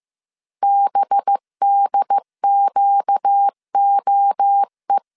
Morse code, heard as a keyed tone, about 15 wpm, 800 Hz. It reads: BDQOE